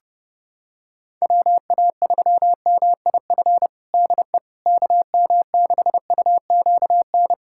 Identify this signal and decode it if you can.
Morse code, heard as a keyed tone, about 30 wpm, 700 Hz. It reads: WA3MIF DE KM6UQN
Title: USA Callsign DE Callsign - 6 Characters - 30wpm